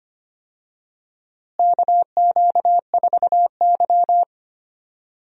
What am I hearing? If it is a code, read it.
Morse code, heard as a keyed tone, about 25 wpm, 700 Hz. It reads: KQ4Y
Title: USA Call Signs - 4 Characters - Repeated 2x - 25wpm